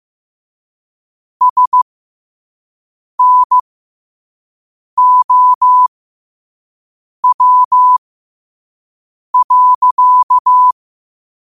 Morse code, heard as a keyed tone, about 15 wpm, spaced as for 5 wpm, 1000 Hz. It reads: SNOW.
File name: qotdfiles / 5wpm